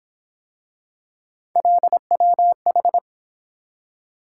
Morse code, heard as a keyed tone, about 28 wpm, 700 Hz. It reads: LWH